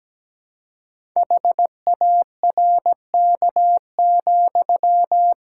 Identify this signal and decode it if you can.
Morse code, heard as a keyed tone, about 17 wpm, 700 Hz. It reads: HARK,
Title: Pangram Challenge v2 - 17wpm